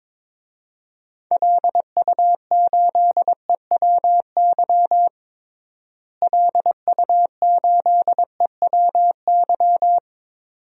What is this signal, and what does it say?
Morse code, heard as a keyed tone, about 22 wpm, 700 Hz. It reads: LU8EWY LU8EWY
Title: DXCC-100 Call Signs - 6 Characters - Repeated 2x - 22wpm